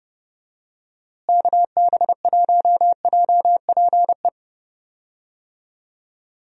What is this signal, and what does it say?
Morse code, heard as a keyed tone, about 32 wpm, 700 Hz. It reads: KB1JPE
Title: USA Call Signs - 6 Characters - Repeated 3x - 30wpm